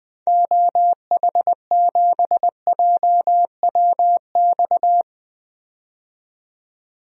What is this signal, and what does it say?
Morse code, heard as a keyed tone, about 20 wpm, 700 Hz. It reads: OH7JWX
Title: DXCC-100 Call Signs - 6 Characters - 20wpm